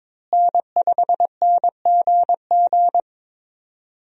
Morse code, heard as a keyed tone, about 21 wpm, 700 Hz. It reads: N5NGG
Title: USA Call Signs - 5 Characters - Repeated 3x - 22wpm